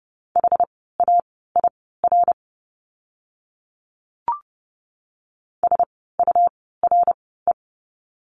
Morse code, heard as a keyed tone, about 30 wpm, spaced as for 17 wpm, 700 Hz. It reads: HAIR SURE
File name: Lesson 17f - Letter u - Words 30wpm